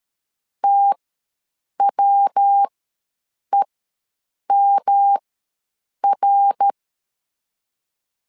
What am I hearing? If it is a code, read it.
Morse code, heard as a keyed tone, about 13 wpm, spaced as for 7 wpm, 800 Hz. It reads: TWEMR
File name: LEZIONI / 3 MWRO / 3 MWR ws 12wpm 1809